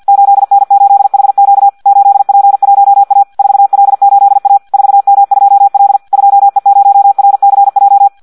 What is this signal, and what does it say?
Morse code, heard as a keyed tone, about 50 wpm, 800 Hz. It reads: 8N8HQ 8P1A 4L8A 4M2L 2E0RCJ
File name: Audio practica CW / qrq-tx-50 2